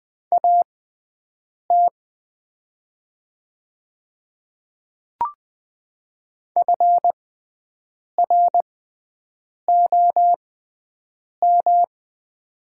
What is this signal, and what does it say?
Morse code, heard as a keyed tone, about 20 wpm, spaced as for 7 wpm, 700 Hz. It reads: AT FROM
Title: 6x Character Spacing - Top 100 Words - 20wpm